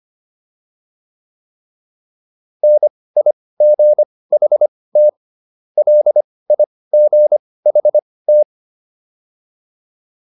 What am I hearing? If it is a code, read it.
Morse code, heard as a keyed tone, about 25 wpm, spaced as for 18 wpm, 600 Hz. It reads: NIGHT LIGHT